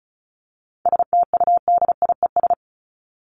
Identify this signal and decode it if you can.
Morse code, heard as a keyed tone, about 34 wpm, 700 Hz. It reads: STUDIES
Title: ES:S Suffix - Top 100 Words with Base Words 35wpm